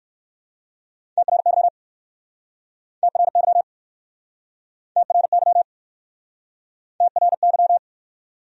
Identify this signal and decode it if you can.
Morse code, heard as a keyed tone, about 67 wpm, 700 Hz. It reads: TRY TRY TRY TRY